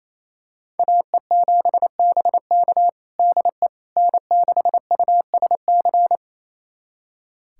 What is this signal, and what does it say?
Morse code, heard as a keyed tone, about 28 wpm, 700 Hz. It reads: AE7BK DE N6USC
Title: USA Callsign DE Callsign - 5 Characters - 28wpm